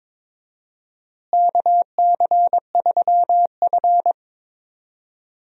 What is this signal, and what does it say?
Morse code, heard as a keyed tone, about 22 wpm, 700 Hz. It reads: KC3F
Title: USA Call Signs - 4 Characters - Repeated 3x - 22wpm